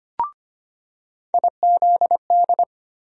Morse code, heard as a keyed tone, about 25 wpm, 700 Hz. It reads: IZD
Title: USA Call Sign Postfixes - 3 Letters 25wpm